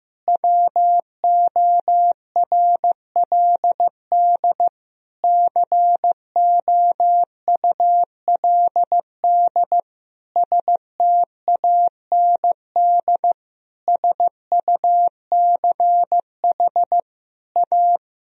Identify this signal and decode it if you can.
Morse code, heard as a keyed tone, about 15 wpm, 700 Hz. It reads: WORLD COULD STAND SUCH A
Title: Sentences from Top 1000 Words - Review Part 3 - 15wpm